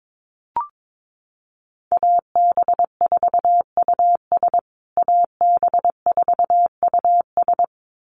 Morse code, heard as a keyed tone, about 22 wpm, 700 Hz. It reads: AB4US AB4US
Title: USA Call Signs Repeated 2x 22wpm